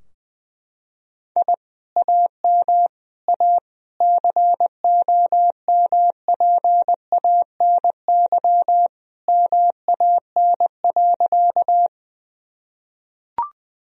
Morse code, heard as a keyed tone, about 20 wpm, 700 Hz. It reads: I AM A COMPANY MAN.
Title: Sentences from Top 1000 Words - Review Part 3 - 20wpm